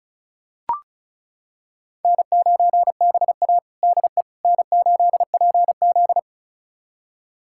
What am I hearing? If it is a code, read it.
Morse code, heard as a keyed tone, about 34 wpm, 700 Hz. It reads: N9BA DE N8PZ
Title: USA Callsign DE Callsign - 4 Characters - 35wpm